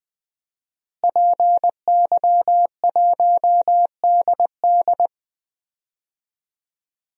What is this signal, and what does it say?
Morse code, heard as a keyed tone, about 19 wpm, 700 Hz. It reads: PY1DD
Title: DXCC-100 Call Signs - 5 Characters - Repeated 3x - 20wpm